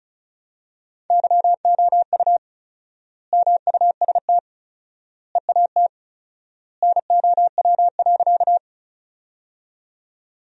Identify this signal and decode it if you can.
Morse code, heard as a keyed tone, about 34 wpm, 700 Hz. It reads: YOU MUST EAT NOW.
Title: 4x Word-Spacing - Sentences from Top 300 Words - 35wpm